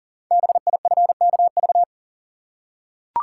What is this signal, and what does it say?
Morse code, heard as a keyed tone, about 42 wpm, 690 Hz. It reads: BIFKV
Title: Code Groups - 5 Letter:Numbers - 40wpm